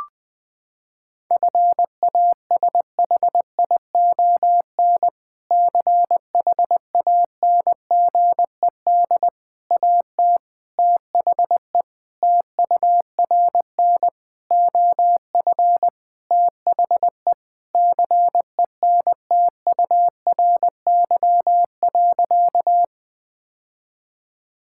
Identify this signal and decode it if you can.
Morse code, heard as a keyed tone, about 21 wpm, 700 Hz. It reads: FASHION CHANGED AT THE TURN OF THE CENTURY.
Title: Sentences with 5-Word Phrases - 20wpm